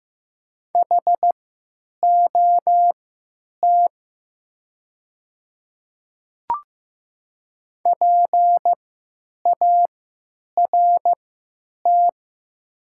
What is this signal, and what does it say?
Morse code, heard as a keyed tone, about 15 wpm, spaced as for 9 wpm, 700 Hz. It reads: HOT PART
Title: 3x Character Spacing - Top 200 Words - 15wpm